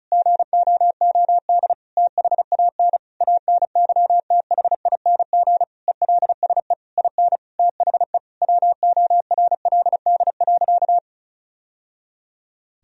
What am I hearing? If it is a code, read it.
Morse code, heard as a keyed tone, about 36 wpm, 700 Hz. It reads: GOOD THAN ANYTHING ELSE IN THE WORLD.
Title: Sentences from Top 1000 Words - Review Part 4 - 35wpm